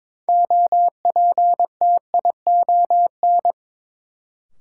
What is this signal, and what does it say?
Morse code, heard as a keyed tone, about 22 wpm, 700 Hz. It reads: OPTION
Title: Sentences from Top 1300 Words - New Word Drill Alt - 22wpm